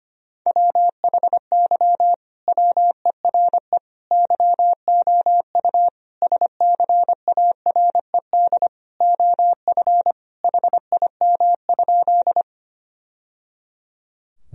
Morse code, heard as a keyed tone, about 25 wpm, 700 Hz. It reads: WHY WERE YOU SCARED OF HIM?
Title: Sentences from Top 1000 Words - Review Part 4 - 25wpm